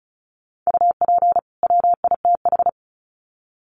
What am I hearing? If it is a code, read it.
Morse code, heard as a keyed tone, about 34 wpm, 700 Hz. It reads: UP WITH
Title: Sets of 2 Words 35wpm